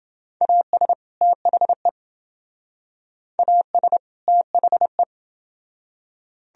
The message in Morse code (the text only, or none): AS THE AS THE